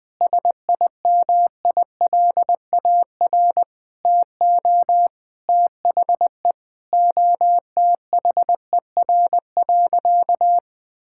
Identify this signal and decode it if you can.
Morse code, heard as a keyed tone, about 21 wpm, 700 Hz. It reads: SIMILAR TO THE OTHER.
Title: Sentences from Top 1000 Words - Review Part 5 - 20wpm